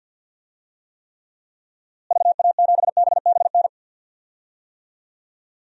Morse code, heard as a keyed tone, about 45 wpm, 700 Hz. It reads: VA7BBN
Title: DXCC-100 Call Signs - 6 Characters - Repeated 3x - 50wpm